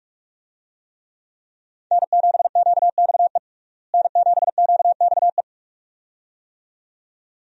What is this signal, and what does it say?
Morse code, heard as a keyed tone, about 46 wpm, 700 Hz. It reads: N7QXE N7QXE
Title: USA Call Signs - 5 Characters - Repeated 2x - 45wpm